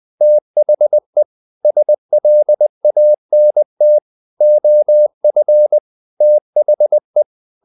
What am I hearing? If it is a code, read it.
Morse code, heard as a keyed tone, about 20 wpm, 600 Hz. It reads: THE SLANT OF THE
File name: the old man and the sea / the old man and the sea part 1 20wpm